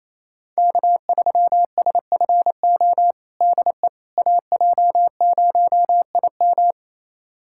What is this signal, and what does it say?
Morse code, heard as a keyed tone, about 28 wpm, 700 Hz. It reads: K3SFO DE AJ0IM
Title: USA Callsign DE Callsign - 5 Characters - 28wpm